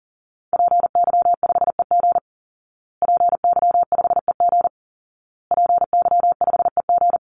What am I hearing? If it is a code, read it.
Morse code, heard as a keyed tone, about 40 wpm, 700 Hz. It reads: PY5EG PY5EG PY5EG
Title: Call Signs Repeated 2x - 40wpm